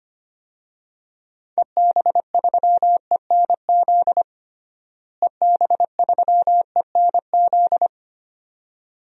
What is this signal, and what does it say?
Morse code, heard as a keyed tone, about 25 wpm, 700 Hz. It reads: EB3ENZ EB3ENZ
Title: DXCC-100 Call Signs - 6 Characters - Repeated 2x - 25wpm